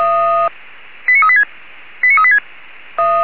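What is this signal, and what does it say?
Five-tone selective call (ZVEI), heard as quick runs of five tones, tones of about 70 ms. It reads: 89386 89386
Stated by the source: Five-tone ZVEI selective call received by SDR, from feuerwehr-orlamuende.org fire brigade audio archive.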